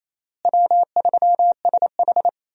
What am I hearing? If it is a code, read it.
Morse code, heard as a keyed tone, about 27 wpm, 700 Hz. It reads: W3SH
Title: USA Call Signs - 4 Characters - Repeated 3x - 28wpm